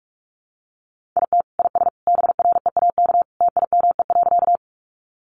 Morse code, heard as a keyed tone, about 44 wpm, 700 Hz. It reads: IT IS BREAK TIME.
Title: Easy Phrases 45wpm